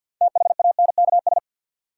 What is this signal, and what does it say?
Morse code, heard as a keyed tone, about 52 wpm, 700 Hz. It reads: THANKS
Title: Sets of 5 Words - New Word Drill - 50wpm